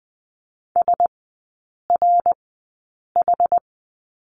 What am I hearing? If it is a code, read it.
Morse code, heard as a keyed tone, about 20 wpm, spaced as for 8 wpm, 700 Hz. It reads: SRH
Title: 067 Lesson 12e - Letter d - Three Characters 20wpm